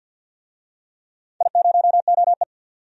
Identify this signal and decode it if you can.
Morse code, heard as a keyed tone, about 52 wpm, 700 Hz. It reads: I0OE
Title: DXCC-100 Call Signs - 4 Characters - 50wpm